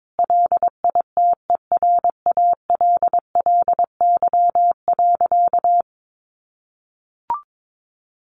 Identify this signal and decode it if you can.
Morse code, heard as a keyed tone, about 22 wpm, 700 Hz. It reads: LITERALLY.
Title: Sentences from Top 1400 Words 22wpm